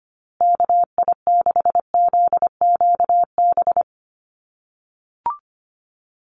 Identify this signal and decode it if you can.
Morse code, heard as a keyed tone, about 25 wpm, 700 Hz. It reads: KI6ZQB
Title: USA Call Signs Repeated 2x 25wpm